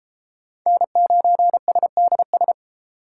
Morse code, heard as a keyed tone, about 29 wpm, 700 Hz. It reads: N9SDS